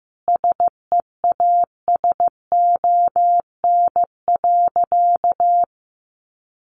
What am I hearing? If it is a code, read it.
Morse code, heard as a keyed tone, about 15 wpm, 700 Hz. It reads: SEASON.
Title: Sentences from Top 1400 Words 15wpm